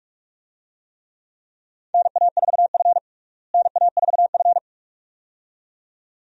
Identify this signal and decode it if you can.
Morse code, heard as a keyed tone, about 42 wpm, 700 Hz. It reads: NA4F NA4F
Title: USA Call Signs - 4 Characters - Repeated 2x - 45wpm